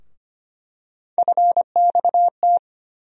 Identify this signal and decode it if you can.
Morse code, heard as a keyed tone, about 25 wpm, 700 Hz. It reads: FXT
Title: USA Call Sign Postfixes - 3 Letters 25wpm